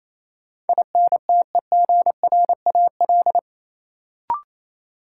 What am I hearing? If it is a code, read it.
Morse code, heard as a keyed tone, about 28 wpm, 700 Hz. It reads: INTEGRAL
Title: Sets of 5 Words - New Word Drill - 28wpm